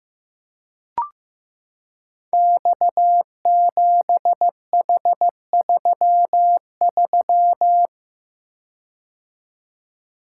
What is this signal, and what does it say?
Morse code, heard as a keyed tone, about 15 wpm, 700 Hz. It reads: X7H33
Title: Code Groups - 5 Letter:Numbers - 15wpm